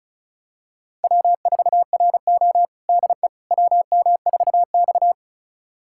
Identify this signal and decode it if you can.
Morse code, heard as a keyed tone, about 34 wpm, 700 Hz. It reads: W4RO DE WM4X
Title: USA Callsign DE Callsign - 4 Characters - 35wpm